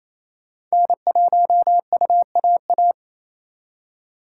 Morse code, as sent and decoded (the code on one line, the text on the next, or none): -. .---- ..- .- .-
N1UAA